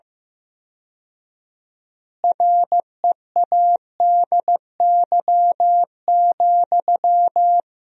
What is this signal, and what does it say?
Morse code, heard as a keyed tone, about 15 wpm, 700 Hz. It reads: READY,
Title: 4x Word-Spacing - Sentences from Top 500 Words - 15wpm